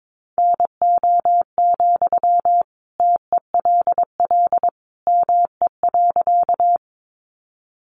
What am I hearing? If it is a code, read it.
Morse code, heard as a keyed tone, about 22 wpm, 700 Hz. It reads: NO, TELL ME.
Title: Sentences from Top 200 Words 22wpm